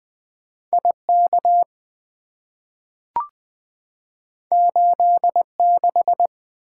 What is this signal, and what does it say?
Morse code, heard as a keyed tone, about 20 wpm, 700 Hz. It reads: IK 86
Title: Two Letter-Number Combinations - 20wpm